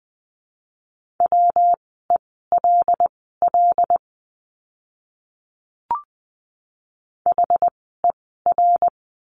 Morse code, heard as a keyed tone, about 20 wpm, spaced as for 14 wpm, 700 Hz. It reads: WELL HER
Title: 192 Lesson 31f - Letter v - Words 20wpm